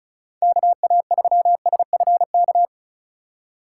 Morse code, heard as a keyed tone, about 34 wpm, 700 Hz. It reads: KA3SFK
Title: USA Call Signs - 6 Characters - 2x Character Spacing - 35wpm